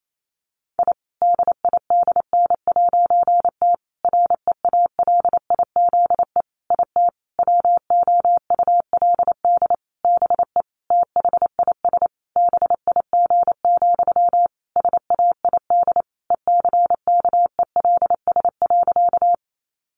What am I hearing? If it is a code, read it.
Morse code, heard as a keyed tone, about 28 wpm, 700 Hz. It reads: I DIDN'T REALIZE IT WOULD BE THIS BIG, SAID ECKELS.
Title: A Sound Of Thunder 28wpm